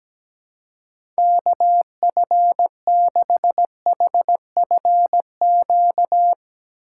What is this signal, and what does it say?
Morse code, heard as a keyed tone, about 17 wpm, 700 Hz. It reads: KF6HFQ
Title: USA Call Signs - 6 Characters - 4x Character Spacing - 17wpm